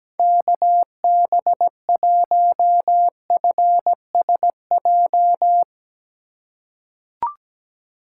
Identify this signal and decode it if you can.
Morse code, heard as a keyed tone, about 17 wpm, 700 Hz. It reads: KB1FSJ
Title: USA Call Signs - 6 Characters - 2x Character Spacing - 17wpm